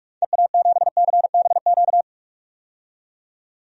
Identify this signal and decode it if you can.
Morse code, heard as a keyed tone, about 41 wpm, 700 Hz. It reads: EA7CBQ
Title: DXCC-100 Call Signs - 6 Characters - 45wpm